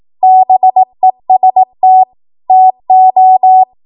Morse code, heard as a keyed tone, about 18 wpm, 750 Hz. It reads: BEST TO
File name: Archive / 18 WPM / 18WPM